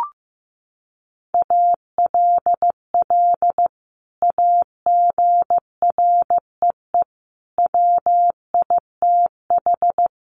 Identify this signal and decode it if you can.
Morse code, heard as a keyed tone, about 15 wpm, 700 Hz. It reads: ALL AGREE WITH
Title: Sentences from Top 1100 Words 15wpm